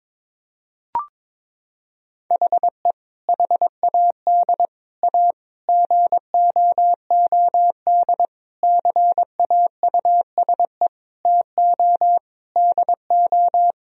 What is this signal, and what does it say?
Morse code, heard as a keyed tone, about 22 wpm, 700 Hz. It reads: HE HAD A GOOD CAUSE TO DO
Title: Sentences from Top 1000 Words - Review Part 1 - 22wpm